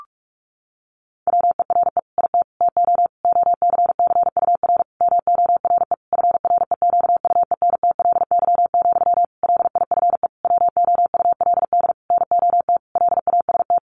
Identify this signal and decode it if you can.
Morse code, heard as a keyed tone, about 44 wpm, 700 Hz. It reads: WERE IT TO OCCUR MORE FREQUENTLY, LIFE WOULD NOT LAST
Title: Sentences from Top 1300 Words 45wpm